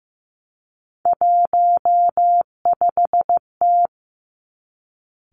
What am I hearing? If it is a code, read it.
Morse code, heard as a keyed tone, about 15 wpm, 700 Hz. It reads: 15T
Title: Three Challenging Letter-Number Combinations 15wpm